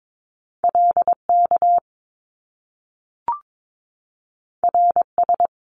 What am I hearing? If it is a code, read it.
Morse code, heard as a keyed tone, about 22 wpm, 700 Hz. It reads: LK RS
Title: Top Two Letter Combinations in Words 22wpm